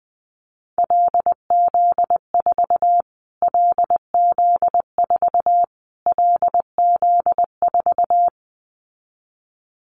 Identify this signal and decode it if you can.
Morse code, heard as a keyed tone, about 20 wpm, 700 Hz. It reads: LZ4 LZ4 LZ4
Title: Callsign Prefixes for Top 25 DXCC - Repeated 3x - 20wpm